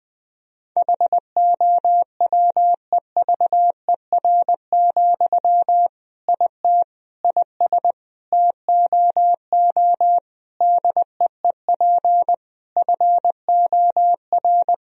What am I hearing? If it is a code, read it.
Morse code, heard as a keyed tone, about 19 wpm, 700 Hz. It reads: HOWEVER, IT IS TOO DEEP FOR
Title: Sentences from Top 1000 Words - Review Part 3 - 20wpm